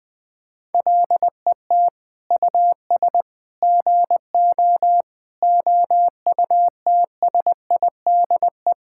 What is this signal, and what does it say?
Morse code, heard as a keyed tone, about 21 wpm, 700 Hz. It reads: LET US GO OUTSIDE
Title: Sentences from Top 1000 Words - Review Part 6 - 20wpm